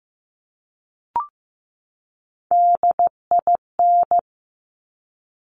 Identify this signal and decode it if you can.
Morse code, heard as a keyed tone, about 15 wpm, 700 Hz. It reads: DIN
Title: Top Three Letter Combinations in Words 15wpm